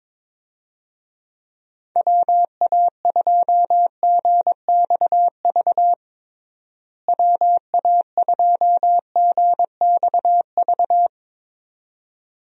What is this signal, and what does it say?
Morse code, heard as a keyed tone, about 22 wpm, 700 Hz. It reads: WA2GXV WA2GXV
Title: USA Call Signs - 6 Characters - Repeated 2x - 22wpm